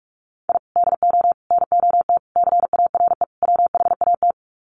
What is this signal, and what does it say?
Morse code, heard as a keyed tone, about 44 wpm, 700 Hz. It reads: I DO NOT CARE WHAT
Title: Sets of 5 Words 45wpm